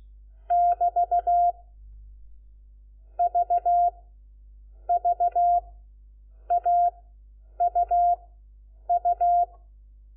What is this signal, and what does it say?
Morse code, heard as a keyed tone, about 16 wpm, 700 Hz. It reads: <BT> V V A U U
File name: Lecciones / CLASE Nº4